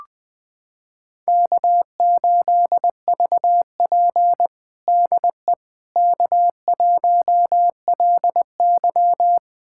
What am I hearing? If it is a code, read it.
Morse code, heard as a keyed tone, about 20 wpm, 700 Hz. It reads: K8VP DE K1LY
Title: USA Callsign DE Callsign - 4 Characters - 20wpm